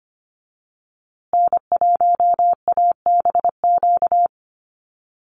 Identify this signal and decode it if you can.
Morse code, heard as a keyed tone, about 25 wpm, 700 Hz. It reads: N1ABQ
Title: USA Call Signs Repeated 3x 25wpm